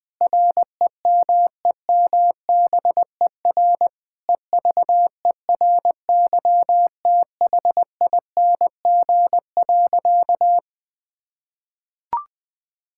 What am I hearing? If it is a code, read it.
Morse code, heard as a keyed tone, about 21 wpm, 700 Hz. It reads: REMEMBER EVERYTHING.
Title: Sentences from Top 1000 Words - Review Part 5 - 20wpm